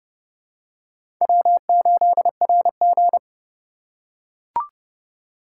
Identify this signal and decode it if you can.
Morse code, heard as a keyed tone, about 30 wpm, 700 Hz. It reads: W8RG
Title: USA Call Signs - 4 Characters - 2x Character Spacing - 30wpm